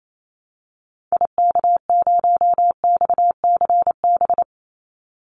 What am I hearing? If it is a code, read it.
Morse code, heard as a keyed tone, about 28 wpm, 700 Hz. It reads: IK0XCB
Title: Call Signs - 28wpm